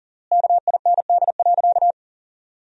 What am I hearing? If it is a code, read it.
Morse code, heard as a keyed tone, about 40 wpm, 700 Hz. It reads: KIND.
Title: 3x Word-Spacing - Sentences from Top 500 Words - 40wpm